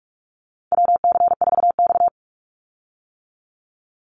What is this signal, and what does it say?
Morse code, heard as a keyed tone, about 44 wpm, 700 Hz. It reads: WC4X
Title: Call Signs - 45wpm